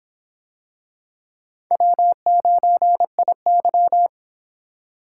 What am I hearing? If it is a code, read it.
Morse code, heard as a keyed tone, about 23 wpm, 700 Hz. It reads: W9IY